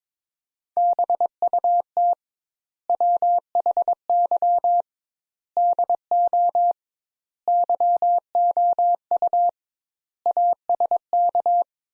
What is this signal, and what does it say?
Morse code, heard as a keyed tone, about 22 wpm, 700 Hz. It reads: BUT WHY DO YOU ASK
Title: Sentences Top 200 Words - 22wpm